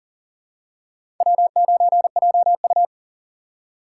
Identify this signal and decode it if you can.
Morse code, heard as a keyed tone, about 37 wpm, 700 Hz. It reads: W9JU
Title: USA Call Signs - 4 Characters - 2x Character Spacing - 40wpm